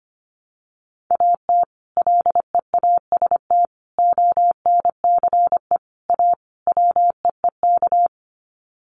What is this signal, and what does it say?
Morse code, heard as a keyed tone, about 25 wpm, 700 Hz. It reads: AT LEAST ONCE A WEEK
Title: Sets of 5 Words Encore2 25wpm